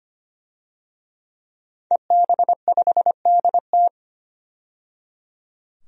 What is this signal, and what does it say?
Morse code, heard as a keyed tone, about 25 wpm, 700 Hz. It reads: EB5DT
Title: DXCC-100 Call Signs - 5 Characters - 25wpm